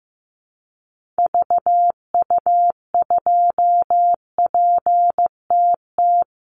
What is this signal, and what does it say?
Morse code, heard as a keyed tone, about 15 wpm, 700 Hz. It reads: VU2PTT
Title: Call Signs Repeated 3x - 15wpm